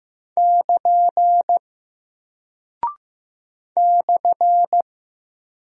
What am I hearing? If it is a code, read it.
Morse code, heard as a keyed tone, about 15 wpm, 700 Hz. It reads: <KN> /
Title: Single Character (Letter:Number:Punctuation:Prosign) Alt 15wpm